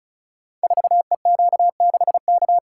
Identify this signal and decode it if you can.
Morse code, heard as a keyed tone, about 36 wpm, 700 Hz. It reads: 4EQ6K
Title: Code Groups - 5 Letter:Numbers - 35wpm